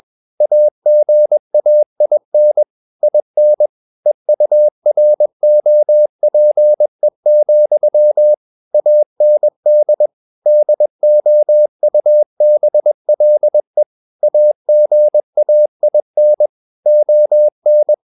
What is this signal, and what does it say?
Morse code, heard as a keyed tone, about 21 wpm, 600 Hz. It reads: AGAIN IN EUROPE, AND DOUBLE AGAIN ON
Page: CW med Gnister